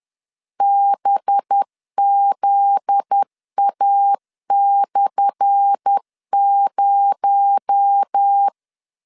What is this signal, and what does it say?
Morse code, heard as a keyed tone, about 10 wpm, 800 Hz. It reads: BZA/0
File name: LEZIONI / 10 SIGNS / 10A SLASH 10wpm 2808